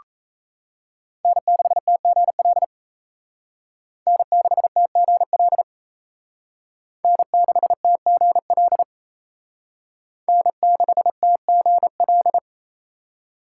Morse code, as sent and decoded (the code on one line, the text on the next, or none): -. -.... - --. .-.. / -. -.... - --. .-.. / -. -.... - --. .-.. / -. -.... - --. .-..
N6TGL N6TGL N6TGL N6TGL